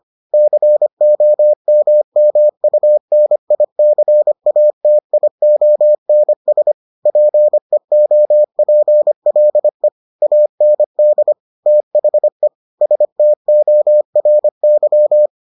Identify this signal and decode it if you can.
Morse code, heard as a keyed tone, about 25 wpm, 600 Hz. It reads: COMMUNICATIONS PEOPLE AND THE STORY